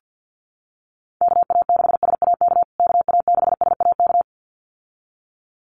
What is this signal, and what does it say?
Morse code, heard as a keyed tone, about 50 wpm, 700 Hz. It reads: KA6SAK KA6SAK
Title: USA Call Signs Repeated 2x 50wpm